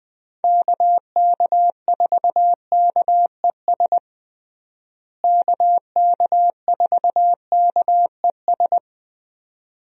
Morse code, heard as a keyed tone, about 21 wpm, 700 Hz. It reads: KK4KES KK4KES
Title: USA Call Signs - 6 Characters - Repeated 2x - 20wpm